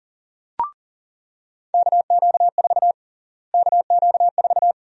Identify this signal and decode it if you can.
Morse code, heard as a keyed tone, about 40 wpm, 700 Hz. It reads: KQ4 KQ4
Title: USA Call Sign Prefixes - Repeated 2x - 40wpm